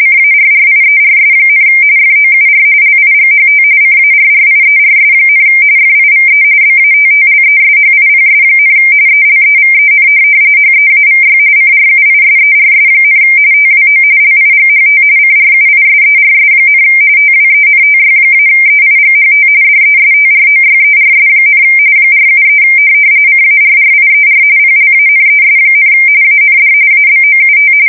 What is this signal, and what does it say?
RTTY, about 45.45 baud. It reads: CARRIERS, FOUR LIGHT CARRIERS, FIVE BATTLESHIPS, ELEVEN HEAVY CRUISERS, TWO LIGHT CRUISERS, TWENTY-EIGHT DESTROYERS, 473 CARRIER PLANES, AND ABOUT
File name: fsk170 / USS-New-Jersey